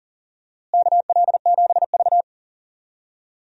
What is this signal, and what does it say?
Morse code, heard as a keyed tone, about 39 wpm, 700 Hz. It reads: KL7V
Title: DXCC-100 Call Signs - 4 Characters - Repeated 2x - 40wpm